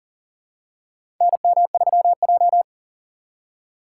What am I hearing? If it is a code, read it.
Morse code, heard as a keyed tone, about 40 wpm, 700 Hz. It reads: NM3J
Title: USA Call Signs - 4 Characters - 2x Character Spacing - 40wpm